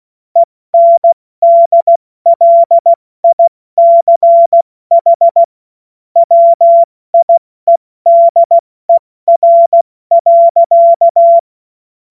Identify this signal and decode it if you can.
Morse code, heard as a keyed tone, about 16 wpm, 680 Hz. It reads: ENDLICH WIEDER.